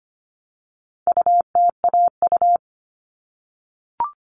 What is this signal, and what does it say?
Morse code, heard as a keyed tone, about 25 wpm, 700 Hz. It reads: UTAU
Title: All Cut Numbers - 4-Digit 25wpm